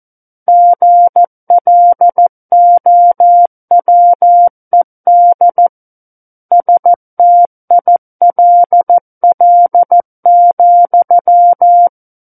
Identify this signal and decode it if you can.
Morse code, heard as a keyed tone, about 14 wpm, 700 Hz. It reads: GLOWED STILL,
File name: Book One / 05-Chapter 5 - 14 WPM